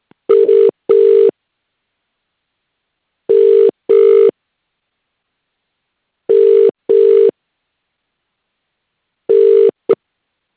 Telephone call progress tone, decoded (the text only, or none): ringback tone